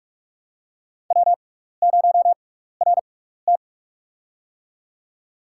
Morse code, heard as a keyed tone, about 44 wpm, spaced as for 15 wpm, 700 Hz. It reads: W0RT
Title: USA Call Signs - 4 Characters - 6x Character Spacing - 45wpm